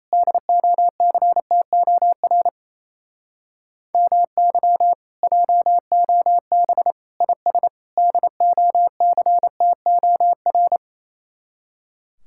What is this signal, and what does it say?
Morse code, heard as a keyed tone, about 34 wpm, 700 Hz. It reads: DOCTOR MY JOB IS DOCTOR
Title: Speed-Racing Common QSO Phrases - 28wpm